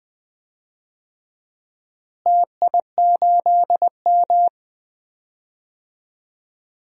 Morse code, heard as a keyed tone, about 20 wpm, 700 Hz. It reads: TI8M
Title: DXCC-100 Call Signs - 4 Characters - Repeated 2x - 20wpm